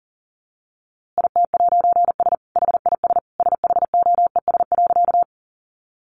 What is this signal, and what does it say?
Morse code, heard as a keyed tone, about 40 wpm, 700 Hz. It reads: IT'S HIS SHOES.
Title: A Sound Of Thunder 40wpm